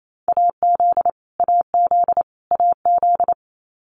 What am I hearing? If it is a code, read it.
Morse code, heard as a keyed tone, about 28 wpm, 700 Hz. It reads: AZ AZ AZ
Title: US States Repeated 3x 28wpm